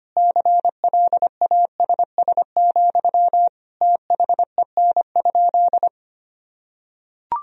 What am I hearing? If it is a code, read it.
Morse code, heard as a keyed tone, about 25 wpm, 700 Hz. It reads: CLASS, THEN?
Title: Sentences from Top 1000 Words - Review Part 4 - 25wpm